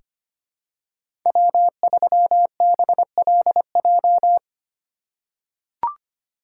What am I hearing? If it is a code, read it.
Morse code, heard as a keyed tone, about 25 wpm, 700 Hz. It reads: W3BLJ
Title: USA Call Signs - 5 Characters - 4x Character Spacing - 25wpm